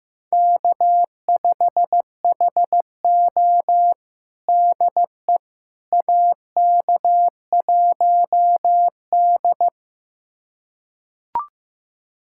Morse code, heard as a keyed tone, about 15 wpm, 700 Hz. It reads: K5HO DE AK1D
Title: USA Callsign DE Callsign - 4 Characters - 15wpm